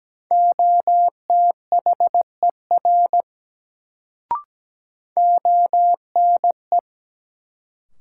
Morse code, heard as a keyed tone, about 17 wpm, 700 Hz. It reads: OTHER ONE
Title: Sentences from Top 100 Words - New Word Drill - 17wpm